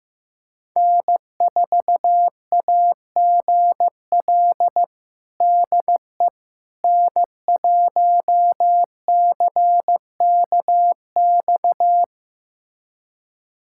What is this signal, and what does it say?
Morse code, heard as a keyed tone, about 15 wpm, 700 Hz. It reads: N4AGL DE N1CKX
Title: USA Callsign DE Callsign - 5 Characters - 15wpm